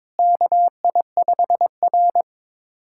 Morse code, heard as a keyed tone, about 22 wpm, 700 Hz. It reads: KI5R
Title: USA Callsign DE Callsign - 4 Characters - 22wpm